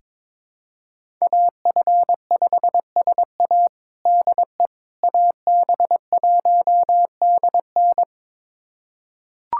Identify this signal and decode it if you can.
Morse code, heard as a keyed tone, about 22 wpm, 700 Hz. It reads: AF5SA DE AB1DN
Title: USA Callsign DE Callsign - 5 Characters - 22wpm